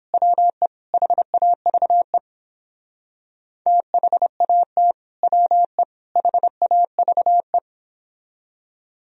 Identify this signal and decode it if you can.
Morse code, heard as a keyed tone, about 31 wpm, 700 Hz. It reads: WE HAVE THAT WE HAVE